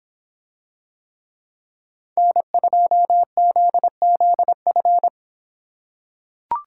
Morse code, heard as a keyed tone, about 26 wpm, 700 Hz. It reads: N2ZZF